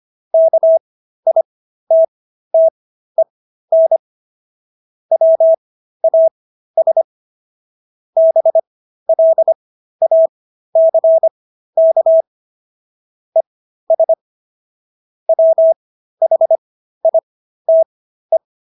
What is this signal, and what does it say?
Morse code, heard as a keyed tone, about 25 wpm, spaced as for 13 wpm, 650 Hz. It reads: KITTEN WAS BLACK ES WHITE